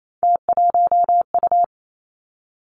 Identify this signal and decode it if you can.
Morse code, heard as a keyed tone, about 28 wpm, 700 Hz. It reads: T1U
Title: Three Challenging Letter-Number Combinations 28wpm